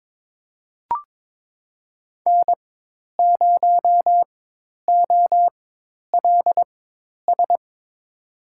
Morse code, heard as a keyed tone, about 22 wpm, spaced as for 10 wpm, 700 Hz. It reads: N0OLS
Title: USA Call Signs - 5 Characters - 4x Character Spacing - 22wpm